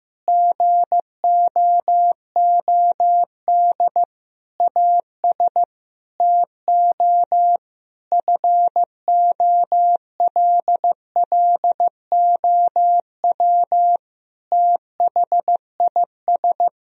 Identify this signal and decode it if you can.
Morse code, heard as a keyed tone, about 15 wpm, 700 Hz. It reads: GOOD AS TO FOLLOW THIS
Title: Sentences from Top 1000 Words - Review Part 1 - 15wpm